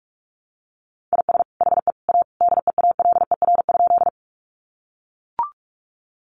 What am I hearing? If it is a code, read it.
Morse code, heard as a keyed tone, about 44 wpm, 700 Hz. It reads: IS HE A DEALER?
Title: Sentences from Top 1100 Words 45wpm